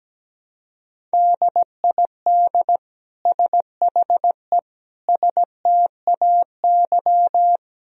Morse code, heard as a keyed tone, about 17 wpm, 700 Hz. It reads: DID SHE STAY
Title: Sentences from Top 500 Words and Abbreviations 17wpm